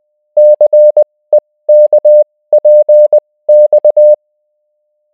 Morse code, heard as a keyed tone, about 20 wpm, spaced as for 16 wpm, 600 Hz. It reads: CEKPX